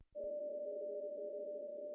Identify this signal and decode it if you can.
PSK31, about 31.25 baud.
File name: CW / cqww-cw-k5tr